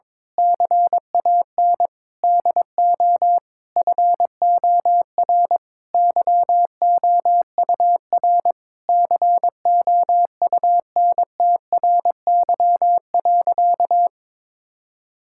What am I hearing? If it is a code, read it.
Morse code, heard as a keyed tone, about 22 wpm, 700 Hz. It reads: CAN DO FOR YOUR COUNTRY.
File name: Sentences with 9-Word Phrases - 22wpm